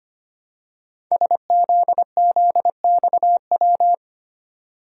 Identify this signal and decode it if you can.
Morse code, heard as a keyed tone, about 25 wpm, 700 Hz. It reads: SZZXW
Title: Code Groups - 5 Letters - 25wpm